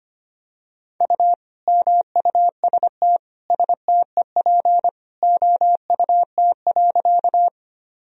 Morse code, heard as a keyed tone, about 25 wpm, 700 Hz. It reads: U MUST STEP OUT.
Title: Sentences from Top 400 Words and Abbreviations 25wpm